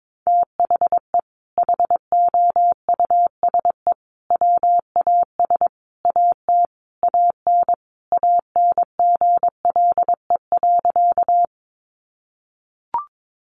Morse code, heard as a keyed tone, about 22 wpm, 700 Hz. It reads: THE HOUSE WAS AT AN ANGLE.
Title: Sentences from Top 1500 Words 22wpm